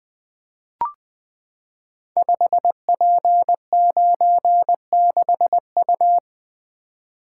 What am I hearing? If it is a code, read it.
Morse code, heard as a keyed tone, about 20 wpm, 700 Hz. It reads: 5P96U
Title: Code Groups - 5 Letter:Numbers - 20wpm